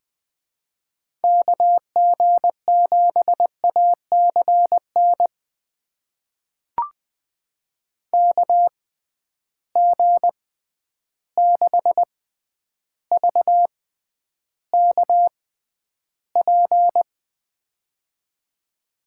Morse code, heard as a keyed tone, about 20 wpm, spaced as for 10 wpm, 700 Hz. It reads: KG7ACN KG6VKP
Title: USA Call Signs - 6 Characters - 6x Character Spacing - 20wpm